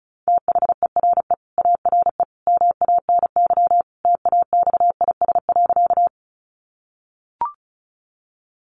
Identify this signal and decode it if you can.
Morse code, heard as a keyed tone, about 34 wpm, 700 Hz. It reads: THERE ARE MANY TAXIS.
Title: Easy Phrases 35wpm